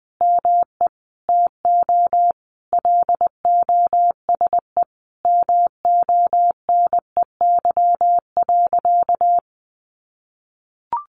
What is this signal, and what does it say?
Morse code, heard as a keyed tone, about 20 wpm, 700 Hz. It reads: ME TO LOSE MONEY.
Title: Sentences from Top 1100 Words 20wpm